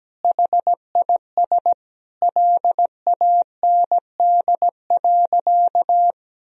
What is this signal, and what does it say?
Morse code, heard as a keyed tone, about 17 wpm, 700 Hz. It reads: HIS LAND.
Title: Sentences from Top 1000 Words - Review Part 4 - 17wpm